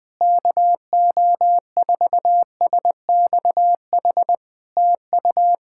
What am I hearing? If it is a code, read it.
Morse code, heard as a keyed tone, about 20 wpm, 700 Hz. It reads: KO4SXH TU
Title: POTA QSO Practice - 20wpm